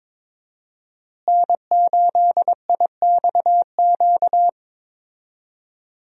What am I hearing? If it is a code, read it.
Morse code, heard as a keyed tone, about 21 wpm, 700 Hz. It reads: N8IXQ
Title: USA Call Signs - 5 Characters - Repeated 2x - 22wpm